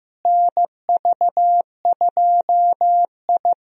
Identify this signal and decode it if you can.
Morse code, heard as a keyed tone, about 16 wpm, 700 Hz. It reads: NV2I
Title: USA Call Signs - 4 Characters - Repeated 2x - 15wpm